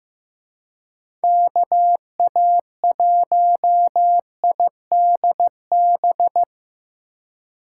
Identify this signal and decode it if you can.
Morse code, heard as a keyed tone, about 17 wpm, 700 Hz. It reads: KA1IDB
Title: Speed Racing - USA Call Signs 15wpm